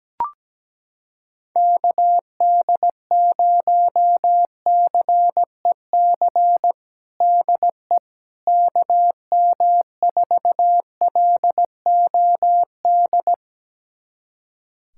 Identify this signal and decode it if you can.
Morse code, heard as a keyed tone, about 17 wpm, 700 Hz. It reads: KD0CEC DE KM4LOD
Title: USA Callsign DE Callsign - 6 Characters - 17wpm